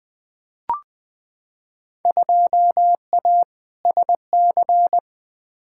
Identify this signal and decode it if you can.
Morse code, heard as a keyed tone, about 21 wpm, 700 Hz. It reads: 2A SC
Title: Field Day Exchanges - 20wpm